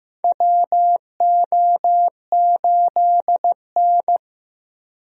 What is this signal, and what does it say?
Morse code, heard as a keyed tone, about 15 wpm, 700 Hz. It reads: WO8N
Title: USA Call Signs - 4 Characters - 15wpm